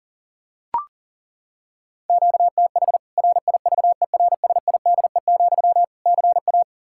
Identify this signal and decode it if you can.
Morse code, heard as a keyed tone, about 40 wpm, 700 Hz. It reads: QTH RIVERSIDE, CA
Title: Practice QSO - Exchange 1 - Beginning - 40wpm